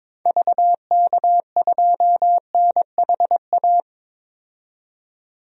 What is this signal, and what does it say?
Morse code, heard as a keyed tone, about 22 wpm, 700 Hz. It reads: VK2NHA
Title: DXCC-100 Call Signs - 6 Characters - 22wpm